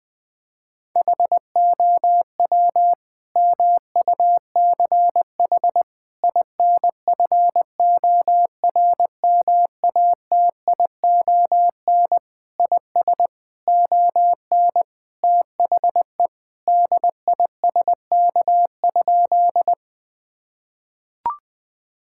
Morse code, heard as a keyed tone, about 20 wpm, 700 Hz. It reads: HOW MUCH INFORMATION IS ON THE DISK?
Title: Sentences from Top 1000 Words - Review Part 6 - 20wpm